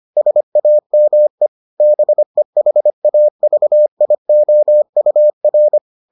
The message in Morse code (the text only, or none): SAME BEHAVIOUR